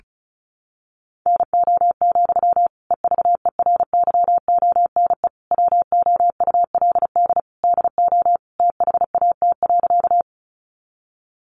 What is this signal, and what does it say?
Morse code, heard as a keyed tone, about 34 wpm, 700 Hz. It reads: NO, EVERYONE WOULD DO THAT.
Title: Sentences from Top 800 Words 35wpm